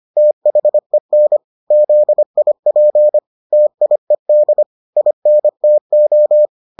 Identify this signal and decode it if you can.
Morse code, heard as a keyed tone, about 25 wpm, 600 Hz. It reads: THEN ZIP TIED INTO